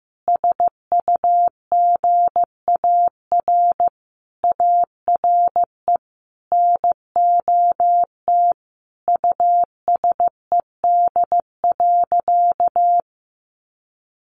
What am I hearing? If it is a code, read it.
Morse code, heard as a keyed tone, about 15 wpm, 700 Hz. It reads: SUGAR ARE NOT USED.
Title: Sentences from Top 1500 Words 15wpm